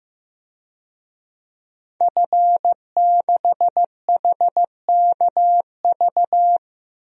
Morse code, heard as a keyed tone, about 15 wpm, 700 Hz. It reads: F6HKV
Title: DXCC-100 Call Signs - 5 Characters - 15wpm